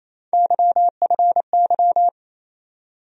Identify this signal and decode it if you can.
Morse code, heard as a keyed tone, about 28 wpm, 700 Hz. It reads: YFY
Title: USA Call Sign Postfixes - 3 Letters 28wpm